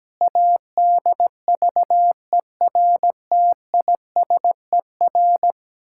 Morse code, heard as a keyed tone, about 17 wpm, 700 Hz. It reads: ADVERTISER
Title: Pangram Challenge v2 - 17wpm